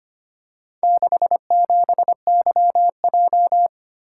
Morse code, heard as a keyed tone, about 25 wpm, 700 Hz. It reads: 67YJ
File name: practice / 25wpm / Four Letter-Number Combinations - 25wpm